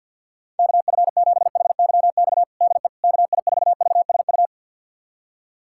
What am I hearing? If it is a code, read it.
Morse code, heard as a keyed tone, about 50 wpm, 690 Hz. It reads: KF7HYX DE KI4VSU
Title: USA Callsign DE Callsign - 6 Characters - 50wpm